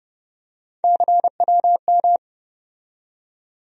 Morse code, heard as a keyed tone, about 30 wpm, 700 Hz. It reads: CWM
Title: USA Call Sign Postfixes - 3 Letters 30wpm